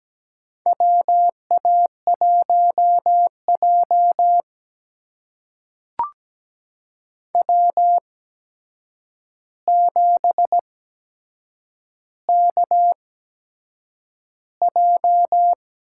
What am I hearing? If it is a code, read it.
Morse code, heard as a keyed tone, about 17 wpm, spaced as for 7 wpm, 700 Hz. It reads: WA1J W7KJ
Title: USA Call Signs - 4 Characters - 8x Character Spacing - 17wpm